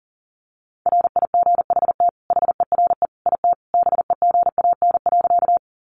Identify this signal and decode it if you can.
Morse code, heard as a keyed tone, about 40 wpm, 700 Hz. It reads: RIGHT HERE IT BEGAN.
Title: Sentences from Top 300 Words 40wpm